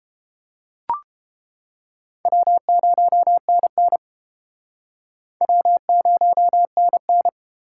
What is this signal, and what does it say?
Morse code, heard as a keyed tone, about 29 wpm, 700 Hz. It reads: W0NN W0NN